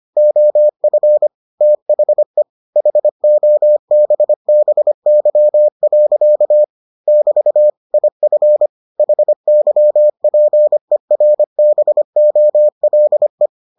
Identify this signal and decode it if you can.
Morse code, heard as a keyed tone, about 25 wpm, 600 Hz. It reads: OF THE HOBBY. <BT> IF HYPERBOLE